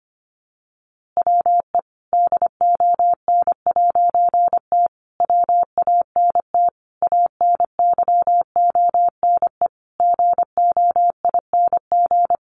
Morse code, heard as a keyed tone, about 25 wpm, 700 Hz. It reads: WE DON'T WANT ANYONE GOING
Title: A Sound Of Thunder 25wpm